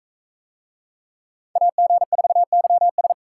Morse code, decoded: AG4YS

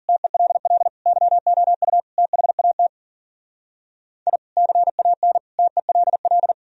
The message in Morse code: TELL YOU THAT I CAN TELL

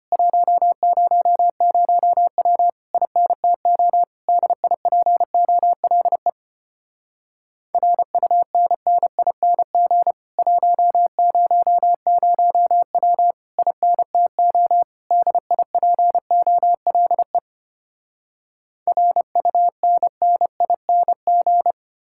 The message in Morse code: 100W INTO DIPOLE RUNNING 100W INTO DIPOLE RUNNING

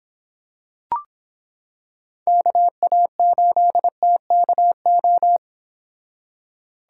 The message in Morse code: KA8TKO

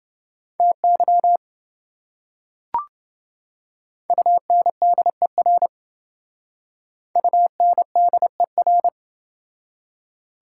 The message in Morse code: TY UNDER UNDER

